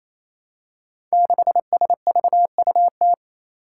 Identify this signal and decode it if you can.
Morse code, heard as a keyed tone, about 29 wpm, 700 Hz. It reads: 6SVUT
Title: Five Challenging Letter-Number Combinations 28wpm